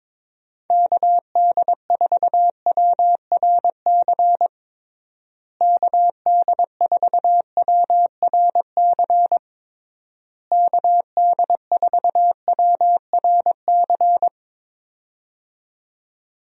Morse code, heard as a keyed tone, about 21 wpm, 700 Hz. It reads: KD4WRC KD4WRC KD4WRC
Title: USA Call Signs - 6 Characters - Repeated 3x - 22wpm